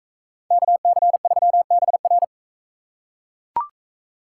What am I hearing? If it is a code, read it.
Morse code, heard as a keyed tone, about 42 wpm, 700 Hz. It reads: KC3BR